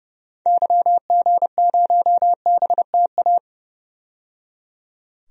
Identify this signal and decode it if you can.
Morse code, heard as a keyed tone, about 30 wpm, 700 Hz. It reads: YG0BTA
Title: DXCC-100 Call Signs - 6 Characters - 30wpm